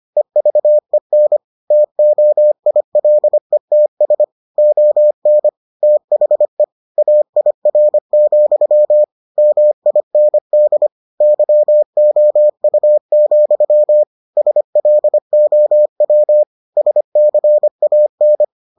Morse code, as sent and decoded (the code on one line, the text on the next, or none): . ...- . -. / - --- .. .-.. - ... / --- -. / - .... / .- .. .-. --..-- / -- .. -. -.. / -.-- --- ..- --..-- / ... .-.. --- .-- / ... -.-. .- -.
EVEN TOILETS ON THE AIR, MIND YOU, SLOW SCAN